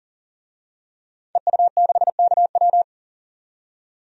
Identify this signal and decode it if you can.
Morse code, heard as a keyed tone, about 40 wpm, 700 Hz. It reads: EU6KW
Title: DXCC-100 Call Signs - 5 Characters - 40wpm